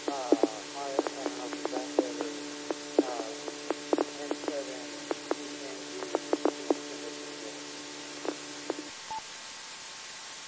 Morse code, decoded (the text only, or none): none